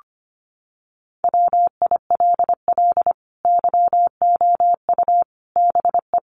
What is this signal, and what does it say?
Morse code, heard as a keyed tone, about 25 wpm, 700 Hz. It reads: WILL YOU BE
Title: Sentences from Top 1100 Words 25wpm